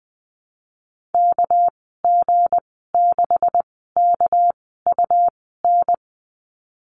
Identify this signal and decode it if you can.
Morse code, heard as a keyed tone, about 20 wpm, spaced as for 15 wpm, 700 Hz. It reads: KG6KUN
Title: 200 Lesson 32g - Letter k - Call Signs 20wpm